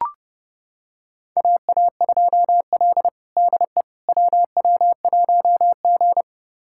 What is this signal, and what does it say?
Morse code, heard as a keyed tone, about 29 wpm, 700 Hz. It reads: AA2L DE WW1G
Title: USA Callsign DE Callsign - 4 Characters - 30wpm